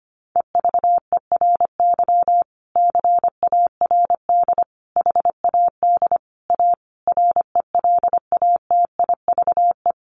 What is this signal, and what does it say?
Morse code, heard as a keyed tone, about 25 wpm, 700 Hz. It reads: EVERY CARD HAD A RELATIVE
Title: Sentences from Top 1400 Words 25wpm